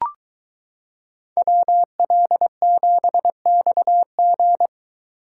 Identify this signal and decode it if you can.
Morse code, heard as a keyed tone, about 23 wpm, 700 Hz. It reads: WL7XG